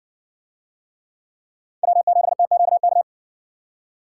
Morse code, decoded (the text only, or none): W7TQK